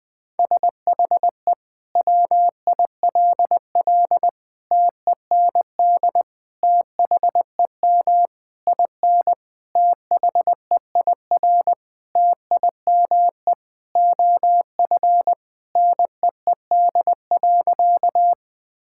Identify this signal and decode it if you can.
Morse code, heard as a keyed tone, about 20 wpm, 700 Hz. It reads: SHE WILL TEND THEM IN THEIR TIME OF NEED.
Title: Sentences from Top 1000 Words - Review Part 5 - 20wpm